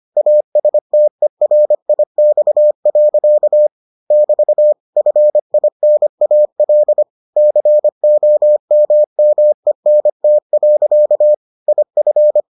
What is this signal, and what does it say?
Morse code, heard as a keyed tone, about 25 wpm, 600 Hz. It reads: ASTERIX. <BT> FINAL COMMENT. IF